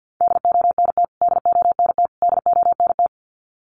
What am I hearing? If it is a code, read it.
Morse code, heard as a keyed tone, about 50 wpm, 700 Hz. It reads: DONT DONT DONT
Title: Top 300 Words Repeated 50wpm